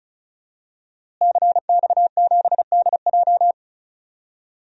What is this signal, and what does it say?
Morse code, heard as a keyed tone, about 34 wpm, 700 Hz. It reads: CX7DJ
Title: DXCC-100 Call Signs - 5 Characters - Repeated 3x - 35wpm